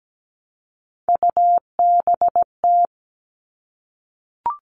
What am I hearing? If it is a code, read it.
Morse code, heard as a keyed tone, about 17 wpm, 700 Hz. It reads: UBT
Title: All Cut Numbers - 3-Digit 17wpm